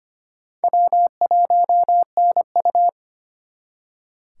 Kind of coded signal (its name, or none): Morse code